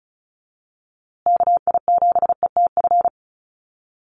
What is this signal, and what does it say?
Morse code, heard as a keyed tone, about 34 wpm, 700 Hz. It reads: KI7ETF
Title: USA Call Signs Repeated 3x 35wpm